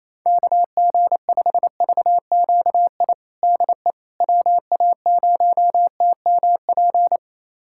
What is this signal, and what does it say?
Morse code, heard as a keyed tone, about 28 wpm, 700 Hz. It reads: KG5VQI DE WA0TMP
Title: USA Callsign DE Callsign 28wpm